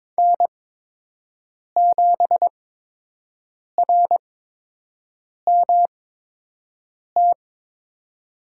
Morse code, heard as a keyed tone, about 22 wpm, spaced as for 6 wpm, 700 Hz. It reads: N7RMT
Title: USA Call Signs - 8x Character Spacing - 22wpm